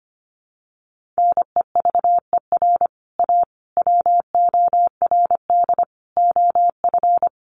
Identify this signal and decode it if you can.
Morse code, heard as a keyed tone, about 25 wpm, 700 Hz. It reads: NEVER A WORD OF
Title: Sentences from Top 1300 Words 25wpm